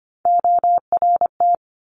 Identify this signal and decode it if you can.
Morse code, heard as a keyed tone, about 25 wpm, 700 Hz. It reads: ORT
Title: Top Three Letter Combinations in Words 25wpm